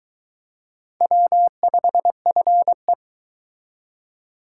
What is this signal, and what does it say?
Morse code, heard as a keyed tone, about 21 wpm, 700 Hz. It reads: W5FE